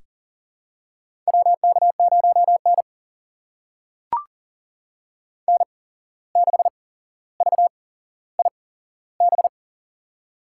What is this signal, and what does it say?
Morse code, heard as a keyed tone, about 39 wpm, spaced as for 16 wpm, 700 Hz. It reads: WK0N N6VIB
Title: USA Call Signs - 8x Character Spacing - 40wpm